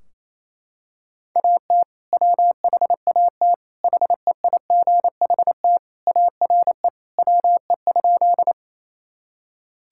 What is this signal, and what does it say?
Morse code, heard as a keyed tone, about 27 wpm, 700 Hz. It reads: AT WHAT HEIGHT ARE WE?
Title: Sentences from Top 1600 Words 28wpm